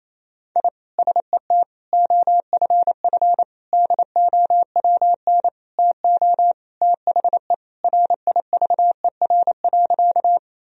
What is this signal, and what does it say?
Morse code, heard as a keyed tone, about 28 wpm, 700 Hz. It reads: I SET OFF DOWN TO THE RIVER.
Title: Sentences from Top 1000 Words - Review Part 1 - 28wpm